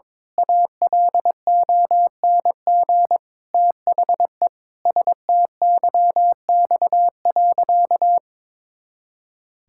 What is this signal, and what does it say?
Morse code, heard as a keyed tone, about 22 wpm, 700 Hz. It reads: ALONG THE STYX.
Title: Pangram Challenge v2 - 22wpm